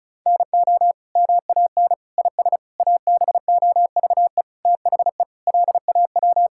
Morse code, heard as a keyed tone, about 36 wpm, 700 Hz. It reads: NO MAN IS ABOVE THE LAW
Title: Sentences from Top 1000 Words - Review Part 5 - 35wpm